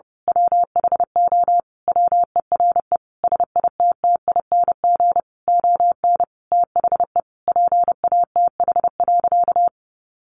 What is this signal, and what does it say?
Morse code, heard as a keyed tone, about 30 wpm, 700 Hz. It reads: WHO WERE SITTING ON THE PATH.
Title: A Sound Of Thunder 30wpm